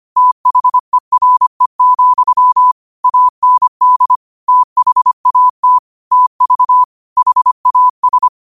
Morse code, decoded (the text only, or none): THERE, AND THAT TV HAS